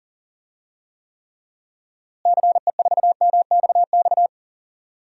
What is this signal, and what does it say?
Morse code, heard as a keyed tone, about 39 wpm, 700 Hz. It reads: CE4MXX
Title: DXCC-100 Call Signs - 6 Characters - Repeated 2x - 40wpm